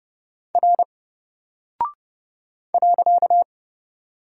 Morse code, heard as a keyed tone, about 30 wpm, 700 Hz. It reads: R .
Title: Single Character (Letter:Number:Punctuation:Prosign) 30wpm